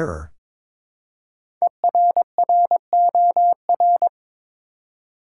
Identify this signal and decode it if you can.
Morse code, heard as a keyed tone, about 22 wpm, 700 Hz. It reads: ERROR
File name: 5 Letter Words - First 3 Letters Given - 22wpm